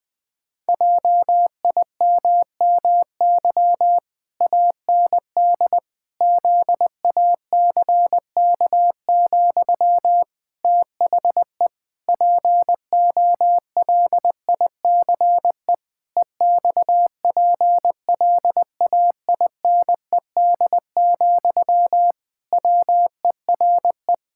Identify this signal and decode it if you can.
Morse code, heard as a keyed tone, about 21 wpm, 700 Hz. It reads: JIMMY AND ZACK, THE POLICE EXPLAINED, WERE
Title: Pangram Challenge v2 - 20wpm